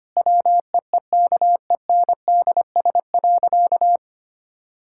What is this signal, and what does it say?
Morse code, heard as a keyed tone, about 25 wpm, 700 Hz. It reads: WEEKENDS.
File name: practice / 25wpm / Difficult Sentences with Hints - 25wpm